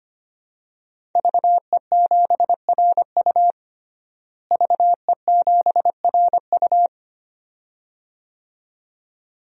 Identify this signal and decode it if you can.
Morse code, heard as a keyed tone, about 25 wpm, 700 Hz. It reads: VE7RU VE7RU
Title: DXCC-100 Call Signs - 5 Characters - Repeated 2x - 25wpm